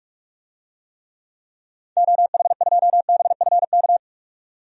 Morse code, heard as a keyed tone, about 46 wpm, 700 Hz. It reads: OH2BFK
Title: DXCC-100 Call Signs - 6 Characters - 45wpm